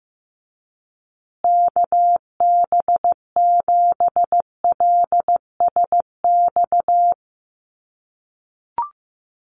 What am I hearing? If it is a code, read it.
Morse code, heard as a keyed tone, about 15 wpm, 700 Hz. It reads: KB7LSX
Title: USA Call Signs Repeated 2x 15wpm